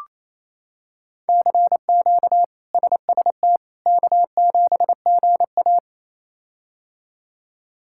Morse code, heard as a keyed tone, about 28 wpm, 700 Hz. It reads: CQ SST K7GA
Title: K1USN SST QSO Practice - 28wpm